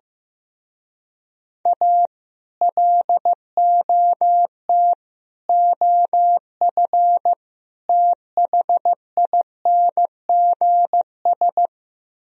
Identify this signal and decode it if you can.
Morse code, heard as a keyed tone, about 15 wpm, 700 Hz. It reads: A LOT OF THINGS
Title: Speed-Racing - Sets of 4 Words 15wpm